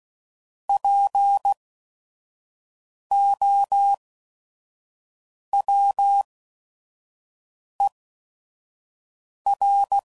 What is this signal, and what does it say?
Morse code, heard as a keyed tone, about 16 wpm, spaced as for 5 wpm, 800 Hz. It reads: POWER
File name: CW Code Groups @5WPM / Technical Subjects / SwPwrSup